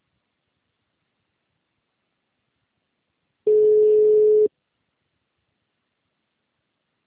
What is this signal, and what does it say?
Ringback tone (Europe, 425 Hz).
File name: out- -2012- - - .48593